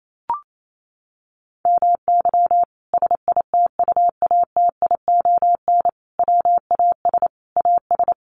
Morse code, heard as a keyed tone, about 28 wpm, 700 Hz. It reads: MY SITUATION WAS AS
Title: Sentences from Top 1200 Words 28wpm